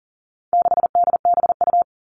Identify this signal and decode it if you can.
Morse code, heard as a keyed tone, about 40 wpm, 700 Hz. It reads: 6DBU
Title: All Cut Numbers - 4-Digit 40wpm